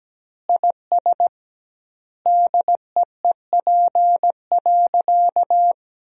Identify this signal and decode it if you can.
Morse code, heard as a keyed tone, about 17 wpm, 700 Hz. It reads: IS DEEP.
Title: 2x Word-Spacing - Sentences from Top 500 Words - 17wpm